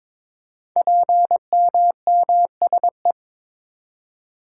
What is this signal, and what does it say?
Morse code, heard as a keyed tone, about 22 wpm, 700 Hz. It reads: PMMSE
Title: Code Groups - 5 Letters - 22wpm